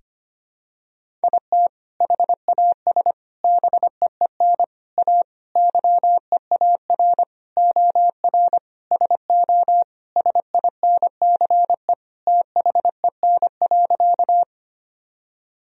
Morse code, heard as a keyed tone, about 25 wpm, 700 Hz. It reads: IT HAS BEEN A YEAR OR SO SINCE THEN.
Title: Sentences from Top 1000 Words - Review Part 1 - 25wpm